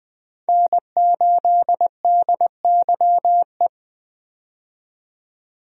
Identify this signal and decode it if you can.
Morse code, heard as a keyed tone, about 21 wpm, 700 Hz. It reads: N8DYE
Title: USA Call Signs - 5 Characters - 20wpm